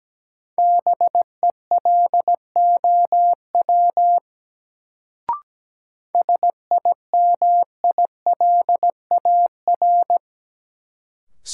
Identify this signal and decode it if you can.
Morse code, heard as a keyed tone, about 17 wpm, 700 Hz. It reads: BELOW SIMILAR
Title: Sentences from Top 1000 Words - Review Part 5 - New Word Drill - 17wpm